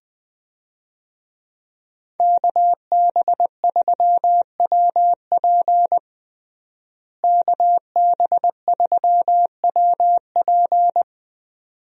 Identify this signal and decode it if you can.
Morse code, heard as a keyed tone, about 20 wpm, 700 Hz. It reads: KB3WP KB3WP
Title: USA Call Signs - 5 Characters - Repeated 2x - 20wpm